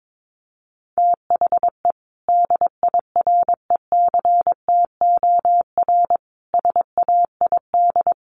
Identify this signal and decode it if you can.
Morse code, heard as a keyed tone, about 22 wpm, 700 Hz. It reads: THE DIRECTOR SAID
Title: Sentences from Top 1300 Words 22wpm